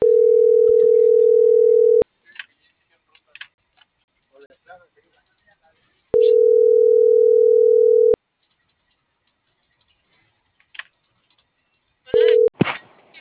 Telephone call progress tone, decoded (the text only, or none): ringback tone